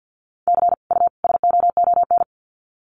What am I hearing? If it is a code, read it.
Morse code, heard as a keyed tone, about 50 wpm, 700 Hz. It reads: C U SOON
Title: Common QSO Phrases 50wpm